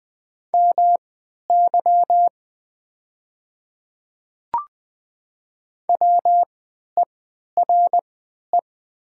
Morse code, heard as a keyed tone, about 20 wpm, spaced as for 11 wpm, 700 Hz. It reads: MY WERE
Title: 3x Character Spacing - Top 100 Words - 20wpm